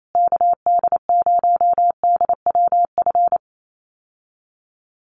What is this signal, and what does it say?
Morse code, heard as a keyed tone, about 28 wpm, 700 Hz. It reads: KD0DWF
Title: USA Call Signs Repeated 2x 28wpm